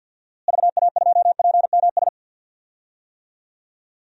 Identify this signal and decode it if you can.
Morse code, heard as a keyed tone, about 48 wpm, 700 Hz. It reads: VA2PMS
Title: DXCC-100 Call Signs - 6 Characters - Repeated 2x - 50wpm